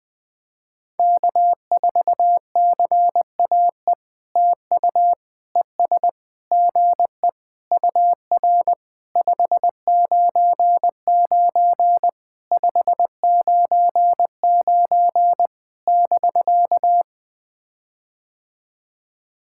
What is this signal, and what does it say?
Morse code, heard as a keyed tone, about 20 wpm, 700 Hz. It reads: K4CAE TU ES GE UR 599 599 <BK>
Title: POTA QSO Practice - 20wpm